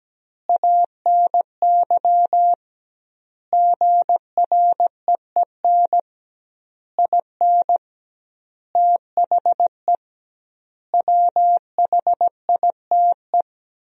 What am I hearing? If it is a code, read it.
Morse code, heard as a keyed tone, about 17 wpm, 700 Hz. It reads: ANY GREEN IN THE WHITE
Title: 2x Word-Spacing - Sentences from Top 500 Words - 17wpm